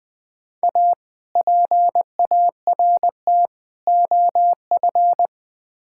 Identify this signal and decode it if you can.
Morse code, heard as a keyed tone, about 20 wpm, 690 Hz. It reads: A PART OF
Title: Build A Phrase - 4-Word Phrases - 20wpm